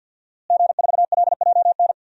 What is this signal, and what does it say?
Morse code, heard as a keyed tone, about 52 wpm, 700 Hz. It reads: G4LJN
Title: DXCC-100 Call Signs - 5 Characters - Repeated 2x - 50wpm